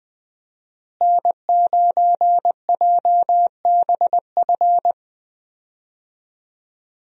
Morse code, heard as a keyed tone, about 20 wpm, 700 Hz. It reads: N9JBF
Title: USA Call Signs - 5 Characters - Repeated 2x - 20wpm